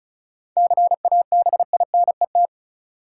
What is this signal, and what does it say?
Morse code, heard as a keyed tone, about 36 wpm, 700 Hz. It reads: CABINET
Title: Seven Letter Words - 35wpm